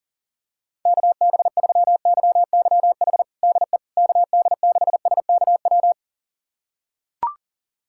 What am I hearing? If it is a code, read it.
Morse code, heard as a keyed tone, about 40 wpm, 700 Hz. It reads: KB3YYH DE KD6SKW
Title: USA Callsign DE Callsign - 6 Characters - 40wpm